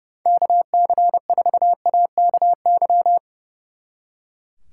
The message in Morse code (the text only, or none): KC4AKY